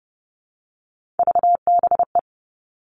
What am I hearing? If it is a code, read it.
Morse code, heard as a keyed tone, about 30 wpm, 700 Hz. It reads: VBE
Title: Three Challenging Letter-Number Combinations 30wpm